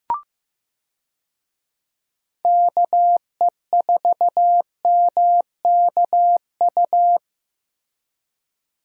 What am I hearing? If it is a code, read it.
Morse code, heard as a keyed tone, about 15 wpm, 700 Hz. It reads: KE4MKU
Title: USA Call Signs - 6 Characters - Repeated 3x - 15wpm